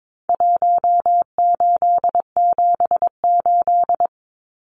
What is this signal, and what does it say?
Morse code, heard as a keyed tone, about 22 wpm, 700 Hz. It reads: 1878
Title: Numbers - Contest Style - 4-Digit 22wpm